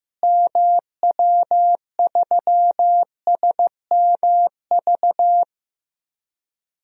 Morse code, heard as a keyed tone, about 15 wpm, 700 Hz. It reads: MW3SMV
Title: DXCC-100 Call Signs - 6 Characters - Repeated 2x - 15wpm